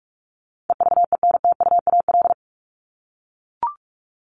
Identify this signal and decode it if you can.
Morse code, heard as a keyed tone, about 44 wpm, 700 Hz. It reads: EVENTUAL
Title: LY Suffix - Top 100 Words with Base Words 45wpm